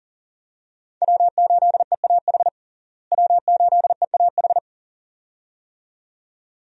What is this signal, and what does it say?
Morse code, heard as a keyed tone, about 40 wpm, 700 Hz. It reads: W8EAH W8EAH
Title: USA Call Signs - 5 Characters - Repeated 2x - 40wpm